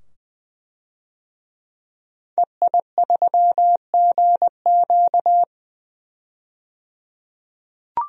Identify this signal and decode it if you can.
Morse code, heard as a keyed tone, about 20 wpm, 700 Hz. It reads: EI3GQ
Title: DXCC-100 Call Signs - 5 Characters - 20wpm